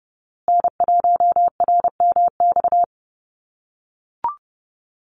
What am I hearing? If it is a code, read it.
Morse code, heard as a keyed tone, about 30 wpm, 700 Hz. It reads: N1RMX
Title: USA Call Signs Repeated 2x 30wpm